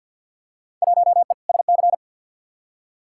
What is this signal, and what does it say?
Morse code, heard as a keyed tone, about 50 wpm, 700 Hz. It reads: 1E SC
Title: Field Day Exchanges - 50wpm